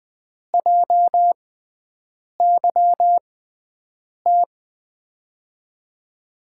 Morse code, heard as a keyed tone, about 21 wpm, spaced as for 7 wpm, 700 Hz. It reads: JYT
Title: USA Call Signs - 6 Characters - 6x Character Spacing - 20wpm